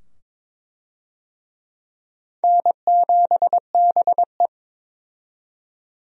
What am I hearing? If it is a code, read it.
Morse code, heard as a keyed tone, about 22 wpm, 700 Hz. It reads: N7BE
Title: USA Call Signs - 4 Characters - 22wpm